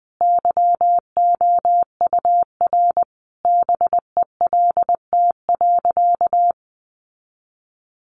Morse code, heard as a keyed tone, about 20 wpm, 700 Hz. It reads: YOUR BELT.
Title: Sentences from Top 1400 Words 20wpm